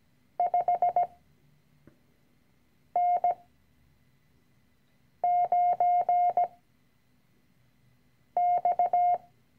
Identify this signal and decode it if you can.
Morse code, heard as a keyed tone, about 17 wpm, 710 Hz. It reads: 5 N 9 X
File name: CW-Online-Kurs / Anfänger / Lektion 010